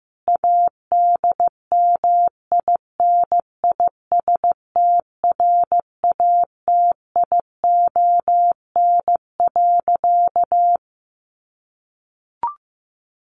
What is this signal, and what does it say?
Morse code, heard as a keyed tone, about 15 wpm, 700 Hz. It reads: ADMINISTRATION.
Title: Sentences from Top 1500 Words 15wpm